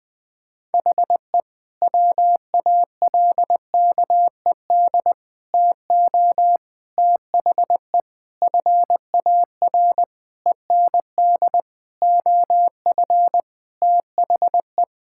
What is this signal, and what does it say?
Morse code, heard as a keyed tone, about 20 wpm, 700 Hz. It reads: HE WALKED TO THE FAR END OF THE
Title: Sentences with 5-Word Phrases - 20wpm